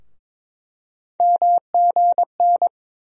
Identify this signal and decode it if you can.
Morse code, heard as a keyed tone, about 22 wpm, 700 Hz. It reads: MGN
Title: USA Call Sign Postfixes - 3 Letters 22wpm